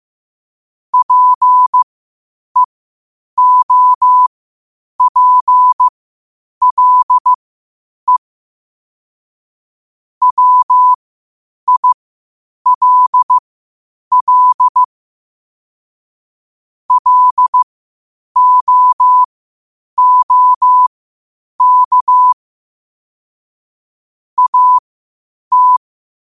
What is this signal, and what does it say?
Morse code, heard as a keyed tone, about 15 wpm, spaced as for 8 wpm, 1000 Hz. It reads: PEOPLE WILL LOOK AT
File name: qotdfiles / 8wpm